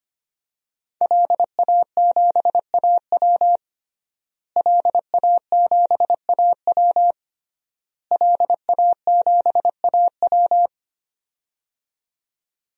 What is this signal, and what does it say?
Morse code, heard as a keyed tone, about 25 wpm, 700 Hz. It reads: LA7AW LA7AW LA7AW
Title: DXCC-100 Call Signs - 5 Characters - Repeated 3x - 25wpm